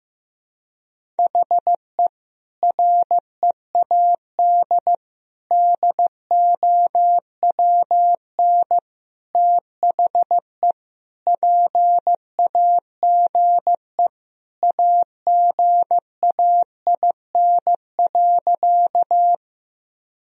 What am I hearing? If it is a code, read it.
Morse code, heard as a keyed tone, about 15 wpm, 700 Hz. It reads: HE READ DOWN THE PAGE AGAIN.
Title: Sentences from Top 1000 Words - Review Part 1 - 15wpm